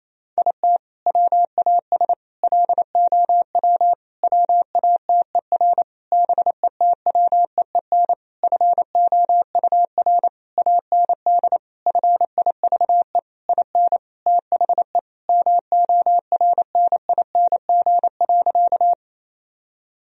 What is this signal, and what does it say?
Morse code, heard as a keyed tone, about 28 wpm, 700 Hz. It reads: IT WAS LOW WATER BETWEEN FOUR AND FIVE IN THE MORNING.
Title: Sentences from Top 1000 Words - Review Part 2 - 28wpm